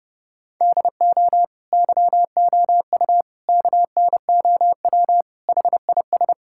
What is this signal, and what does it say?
Morse code, heard as a keyed tone, about 30 wpm, 700 Hz. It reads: DO YOU KNOW HIS
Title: Sentences from Top 1000 Words - Review Part 6 - 30wpm